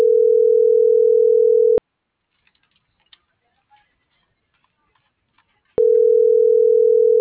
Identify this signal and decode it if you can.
Ringback tone (North America), 440 + 480 Hz.